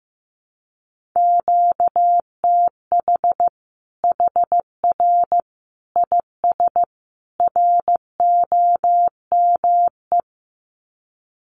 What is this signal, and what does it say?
Morse code, heard as a keyed tone, about 15 wpm, 700 Hz. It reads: QTH HR IS ROME
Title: Common QSO Phrases 15wpm